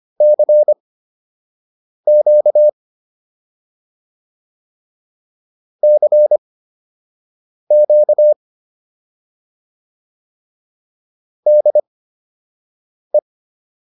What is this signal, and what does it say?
Morse code, heard as a keyed tone, about 25 wpm, spaced as for 6 wpm, 600 Hz. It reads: CQ CQ DE
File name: QSO 105 06